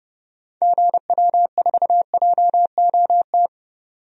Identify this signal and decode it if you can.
Morse code, heard as a keyed tone, about 30 wpm, 700 Hz. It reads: GW4JOT
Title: DXCC-100 Call Signs - 6 Characters - 30wpm